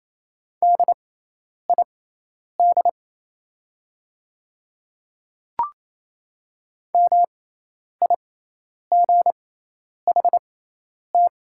Morse code, heard as a keyed tone, about 28 wpm, spaced as for 10 wpm, 700 Hz. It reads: DID MIGHT
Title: 6x Character Spacing - Top 500 Words - 28wpm